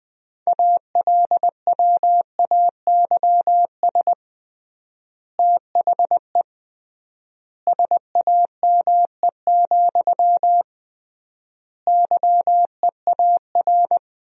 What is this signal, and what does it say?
Morse code, heard as a keyed tone, about 20 wpm, 700 Hz. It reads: ALWAYS THE SAME, YEAR
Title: 3x Word-Spacing - Sentences from Top 300 Words - 20wpm